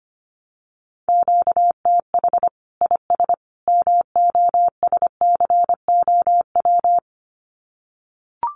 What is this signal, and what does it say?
Morse code, heard as a keyed tone, about 25 wpm, 700 Hz. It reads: QTH IS MOSCOW
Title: Common QSO Phrases 25wpm